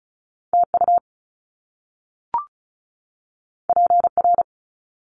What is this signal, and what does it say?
Morse code, heard as a keyed tone, about 36 wpm, 700 Hz. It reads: TU PR
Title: Top Two Letter Combinations in Words 35wpm